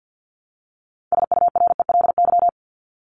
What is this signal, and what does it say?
Morse code, heard as a keyed tone, about 50 wpm, 700 Hz. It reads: SURELY
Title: LY Suffix - Top 100 Words with Base Words 50wpm